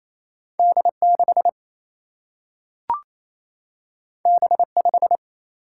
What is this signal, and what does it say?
Morse code, heard as a keyed tone, about 28 wpm, 700 Hz. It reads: D6 B5
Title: Single-Letter Single-Number 28wpm